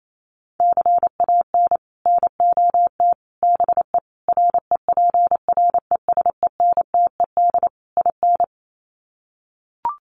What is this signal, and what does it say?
Morse code, heard as a keyed tone, about 28 wpm, 700 Hz. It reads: CAN NOT BE REPRESENTED IN
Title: Sets of 5 Words 28wpm